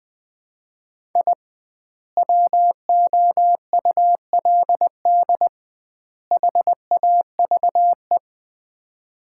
Morse code, heard as a keyed tone, about 20 wpm, 700 Hz. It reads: I WOULD HAVE
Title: 2x Word-Spacing - Sets of 3 Words - 20wpm